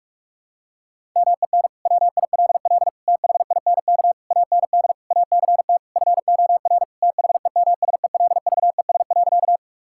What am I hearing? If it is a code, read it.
Morse code, heard as a keyed tone, about 41 wpm, 700 Hz. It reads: MEN WILL THINK AND ACT FOR THEMSELVES.
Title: Sentences from Top 1000 Words - Review Part 5 - 45wpm